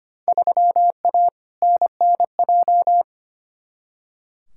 Morse code, heard as a keyed tone, about 25 wpm, 700 Hz. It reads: 3A NNJ
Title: Field Day Exchanges Encore - 25wpm